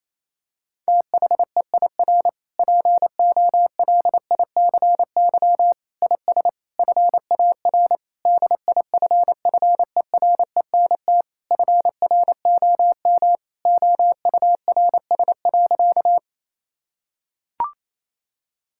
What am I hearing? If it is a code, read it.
Morse code, heard as a keyed tone, about 28 wpm, 700 Hz. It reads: THEIR POLICY IS FAR DIFFERENT FROM OURS.
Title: Sentences from Top 1000 Words - Review Part 5 - 28wpm